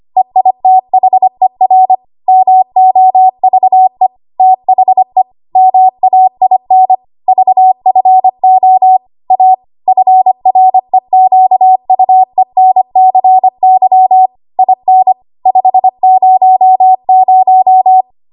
Morse code, heard as a keyed tone, about 25 wpm, 750 Hz. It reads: EITHER MOVE THE MAIN VFO A FREQUENCY IN 500
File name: Archive / 25 WPM / 25WPM